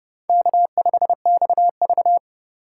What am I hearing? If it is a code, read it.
Morse code, heard as a keyed tone, about 30 wpm, 700 Hz. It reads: K5XV
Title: USA Call Signs - 3x Character Spacing - 30wpm